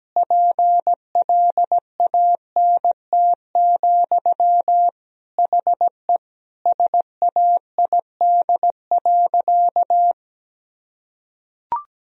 Morse code, heard as a keyed tone, about 17 wpm, 700 Hz. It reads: PLANT, HE SAID.
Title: Sentences from Top 1000 Words - Review Part 1 - 17wpm